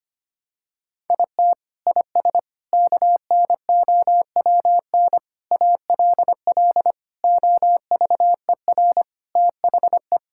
Morse code, heard as a keyed tone, about 25 wpm, 700 Hz. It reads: IT IS KNOWN ALL OVER THE
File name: practice / 25wpm / Sentences with 4-Word Phrases - 25wpm